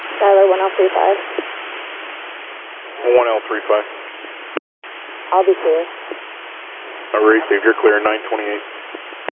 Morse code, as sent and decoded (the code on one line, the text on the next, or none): none